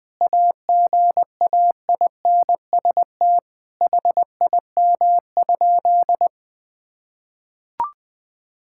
Morse code, heard as a keyed tone, about 21 wpm, 700 Hz. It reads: AGAINST HIM?
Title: Sentences from Top 1000 Words - Review Part 6 - 20wpm